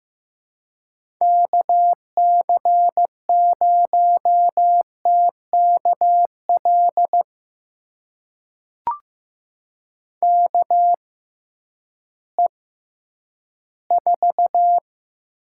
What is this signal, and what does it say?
Morse code, heard as a keyed tone, about 16 wpm, spaced as for 10 wpm, 700 Hz. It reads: KC0TKL KE4
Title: USA Call Signs - 6 Characters - 6x Character Spacing - 15wpm